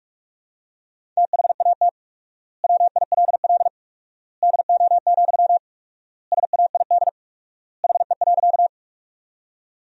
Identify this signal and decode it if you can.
Morse code, heard as a keyed tone, about 46 wpm, 700 Hz. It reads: THAT WILL DO, SAID HE.
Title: 4x Word-Spacing - Sentences from Top 100 Words - 45wpm